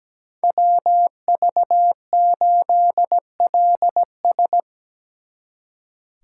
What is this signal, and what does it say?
Morse code, heard as a keyed tone, about 19 wpm, 700 Hz. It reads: WV8LS
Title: Speed Racing - USA Call Signs 17wpm